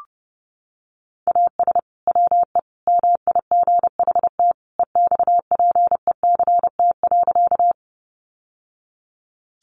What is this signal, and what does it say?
Morse code, heard as a keyed tone, about 28 wpm, 700 Hz. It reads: AS WE MIGHT EXPECT.
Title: Sentences from Top 1100 Words 30wpm